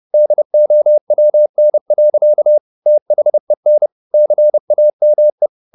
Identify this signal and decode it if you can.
Morse code, heard as a keyed tone, about 30 wpm, 600 Hz. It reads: DOWN. THEN CAME
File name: the old man and the sea / the old man and the sea 1 30wpm